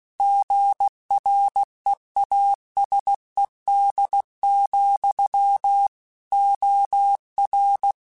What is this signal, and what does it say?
Morse code, heard as a keyed tone, about 16 wpm, 800 Hz. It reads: GREASED, OR
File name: Chuckles / Chuckles-mp3s / Squeaky @15